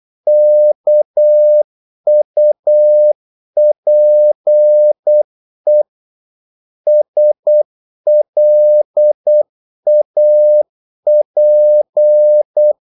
Morse code, heard as a keyed tone, about 8 wpm, 600 Hz. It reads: KUPE SLAP
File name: Korte ord DK 8wpm